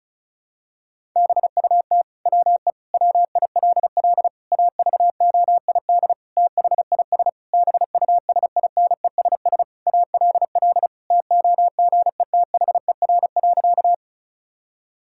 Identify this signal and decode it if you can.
Morse code, heard as a keyed tone, about 36 wpm, 700 Hz. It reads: BUT WE WILL AVOID THIS BUSINESS ALL TOGETHER.
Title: Sentences from Top 1000 Words - Review Part 6 - 35wpm